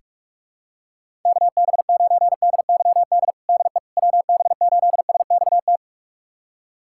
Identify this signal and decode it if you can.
Morse code, heard as a keyed tone, about 46 wpm, 700 Hz. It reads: KB9DYD DE WB8SXT
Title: USA Callsign DE Callsign - 6 Characters - 45wpm